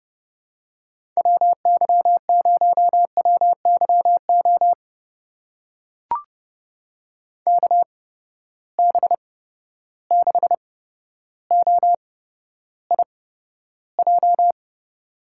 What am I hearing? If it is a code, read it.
Morse code, heard as a keyed tone, about 31 wpm, spaced as for 13 wpm, 700 Hz. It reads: WY0WYO KB6OIJ
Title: USA Call Signs - 6 Characters - 8x Character Spacing - 30wpm